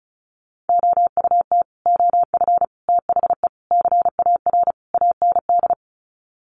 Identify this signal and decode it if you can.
Morse code, heard as a keyed tone, about 34 wpm, 700 Hz. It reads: OUT OF THE CAR AND
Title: Sets of 5 Words Encore 35wpm